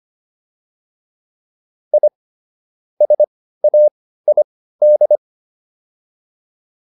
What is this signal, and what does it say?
Morse code, heard as a keyed tone, about 25 wpm, spaced as for 15 wpm, 600 Hz. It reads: I SAID